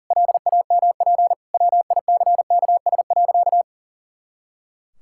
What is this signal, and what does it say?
Morse code, heard as a keyed tone, about 40 wpm, 700 Hz. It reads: LAMP WICKS.
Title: Pangram Challenge v2 - 40wpm